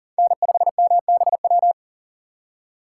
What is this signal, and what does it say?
Morse code, heard as a keyed tone, about 40 wpm, 700 Hz. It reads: N5MBW
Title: USA Call Signs - 5 Characters - 2x Character Spacing - 40wpm